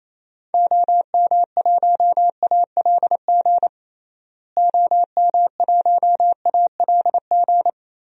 Morse code, heard as a keyed tone, about 28 wpm, 700 Hz. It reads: OM1ALG OM1ALG
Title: DXCC-100 Call Signs - 6 Characters - Repeated 2x - 28wpm